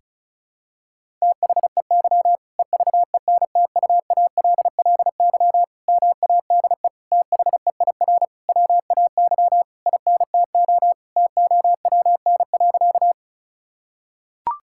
Morse code, heard as a keyed tone, about 36 wpm, 700 Hz. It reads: THEY EVENTUALLY MADE THEIR WAY INTO TOWN.
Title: Sentences from Top 1000 Words - Review Part 6 - 35wpm